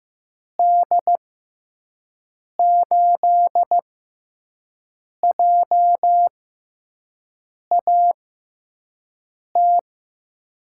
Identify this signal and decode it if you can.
Morse code, heard as a keyed tone, about 15 wpm, spaced as for 5 wpm, 700 Hz. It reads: D8JAT
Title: USA Call Signs - 6 Characters - 6x Character Spacing - 15wpm